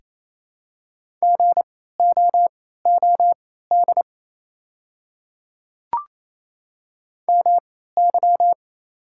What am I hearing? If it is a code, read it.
Morse code, heard as a keyed tone, about 27 wpm, spaced as for 16 wpm, 700 Hz. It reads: GOOD MY
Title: 3x Character Spacing - Top 500 Words - 28wpm